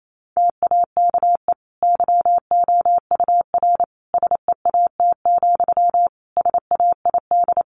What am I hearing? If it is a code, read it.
Morse code, heard as a keyed tone, about 28 wpm, 700 Hz. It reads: TAKE YOUR SEAT, SAID
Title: Sentences from Top 1500 Words 28wpm